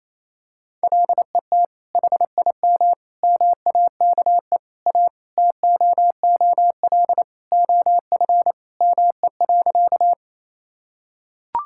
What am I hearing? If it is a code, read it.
Morse code, heard as a keyed tone, about 28 wpm, 700 Hz. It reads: LET HIM MAKE A TOOL OF ME.
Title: Sentences from Top 1000 Words - Review Part 6 - 28wpm